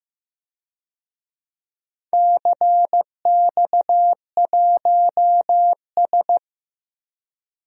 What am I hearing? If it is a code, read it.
Morse code, heard as a keyed tone, about 16 wpm, 700 Hz. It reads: CX1S
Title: DXCC-100 Call Signs - 4 Characters - Repeated 3x - 15wpm